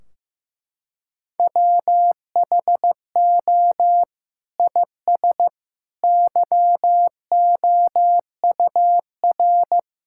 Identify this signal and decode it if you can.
Morse code, heard as a keyed tone, about 15 wpm, 700 Hz. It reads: WHO IS YOUR
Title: Sentences from Top 1600 Words 15wpm